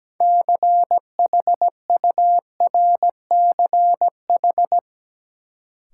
Morse code, heard as a keyed tone, about 17 wpm, 700 Hz. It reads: CHURCH
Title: Sets of 5 Words - New Word Drill - 17wpm